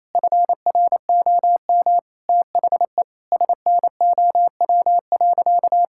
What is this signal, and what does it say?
Morse code, heard as a keyed tone, about 28 wpm, 700 Hz. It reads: FROM THE SNOW.
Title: Sentences from Top 1000 Words - Review Part 5 - 28wpm